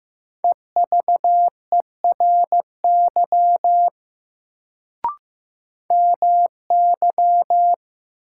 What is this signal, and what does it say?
Morse code, heard as a keyed tone, about 15 wpm, 700 Hz. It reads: EVERY MY
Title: Sentences from Top 200 Words - New Word Drill - 15wpm